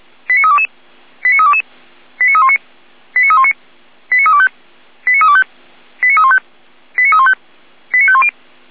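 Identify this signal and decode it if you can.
Five-tone selective call (ZVEI), heard as quick runs of five tones, tones of about 80 ms. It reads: 89322 89322 89319 89319 89325 89325 89315 89315 89510
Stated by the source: Five-tone ZVEI selective call received by SDR.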